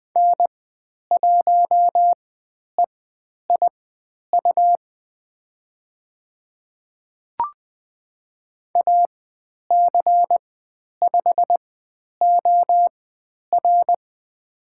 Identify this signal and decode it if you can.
Morse code, heard as a keyed tone, about 20 wpm, spaced as for 10 wpm, 700 Hz. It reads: N1EIU AC5OR
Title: 109 Lesson 18g - Letter m - Call Signs 20wpm